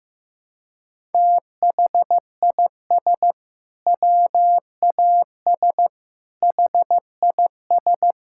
Morse code, heard as a keyed tone, about 16 wpm, 700 Hz. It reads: THIS WAS HIS
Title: Sentences from Top 1000 Words - Review Part 6 - 15wpm